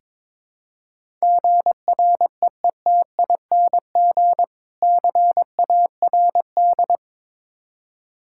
Morse code, heard as a keyed tone, about 22 wpm, 690 Hz. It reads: GREETING CARD
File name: Difficult Words with Hints - 22wpm